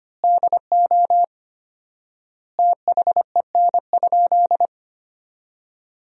Morse code, heard as a keyed tone, about 25 wpm, 700 Hz. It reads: DO THEN?
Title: 4x Word-Spacing - Sentences from Top 100 Words - 25wpm